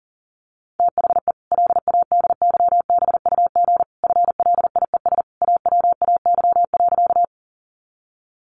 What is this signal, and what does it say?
Morse code, heard as a keyed tone, about 40 wpm, 700 Hz. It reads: THE LADYBUG FLIES AWAY.
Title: Additional Easy Phrases 40wpm